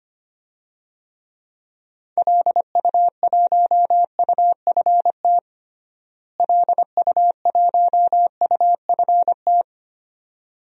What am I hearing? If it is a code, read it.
Morse code, heard as a keyed tone, about 25 wpm, 700 Hz. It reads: LU1UFT LU1UFT
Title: DXCC-100 Call Signs - 6 Characters - Repeated 2x - 25wpm